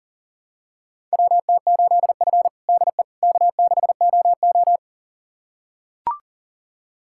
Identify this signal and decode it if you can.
Morse code, heard as a keyed tone, about 40 wpm, 700 Hz. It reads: WT8F DE K6OO
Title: USA Callsign DE Callsign - 4 Characters - 40wpm